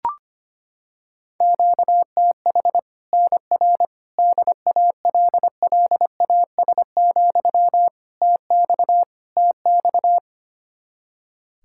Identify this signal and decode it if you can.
Morse code, heard as a keyed tone, about 25 wpm, 700 Hz. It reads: QTH NR DALLAS, TX TX
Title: Practice QSO - Exchange 1 - Beginning - 25wpm